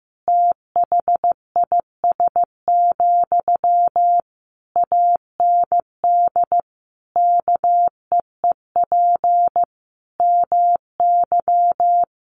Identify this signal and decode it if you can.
Morse code, heard as a keyed tone, about 15 wpm, 700 Hz. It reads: THIS, AND KEEP MY
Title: Sentences from Top 1500 Words 15wpm